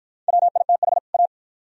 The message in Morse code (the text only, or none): WITH A